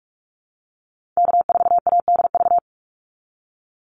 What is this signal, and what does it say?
Morse code, heard as a keyed tone, about 44 wpm, 700 Hz. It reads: K4ADV
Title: NA SOTA Activator Callsigns 45wpm